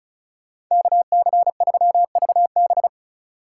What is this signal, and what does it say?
Morse code, heard as a keyed tone, about 44 wpm, 700 Hz. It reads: KC3VB
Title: Speed Racing - USA 2x2 Call Signs 35wpm